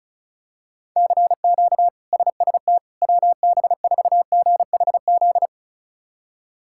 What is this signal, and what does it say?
Morse code, heard as a keyed tone, about 34 wpm, 700 Hz. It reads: CQ SST WB4GHZ
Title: K1USN SST QSO Practice - 35wpm